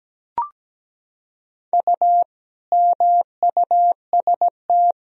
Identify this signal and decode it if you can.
Morse code, heard as a keyed tone, about 17 wpm, 700 Hz. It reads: U MUST
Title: Sentences from Top 400 Words and Abbreviations 17wpm